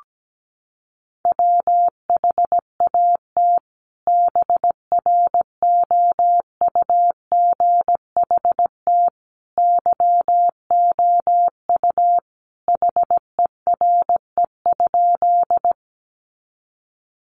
Morse code, heard as a keyed tone, about 17 wpm, 700 Hz. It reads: WHAT BROUGHT YOU HERE?
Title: Sentences from Top 500 Words 17wpm